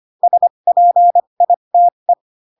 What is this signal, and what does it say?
Morse code, heard as a keyed tone, about 25 wpm, spaced as for 22 wpm, 700 Hz. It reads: SPITE
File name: Pinocchio / Pinocchio ch6 18 25 ik1hge